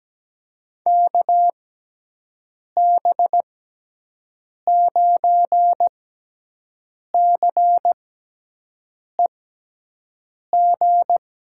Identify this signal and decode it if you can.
Morse code, heard as a keyed tone, about 17 wpm, spaced as for 6 wpm, 700 Hz. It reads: KB9CEG
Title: USA Call Signs - 6 Characters - 6x Character Spacing - 17wpm